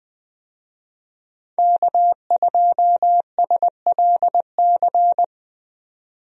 Morse code, heard as a keyed tone, about 20 wpm, 700 Hz. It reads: K2SLC
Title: USA Call Signs - 5 Characters - 20wpm